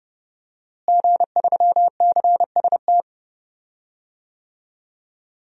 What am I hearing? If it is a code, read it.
Morse code, heard as a keyed tone, about 32 wpm, 700 Hz. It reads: G3CST
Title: DXCC-100 Call Signs - 5 Characters - Repeated 3x - 30wpm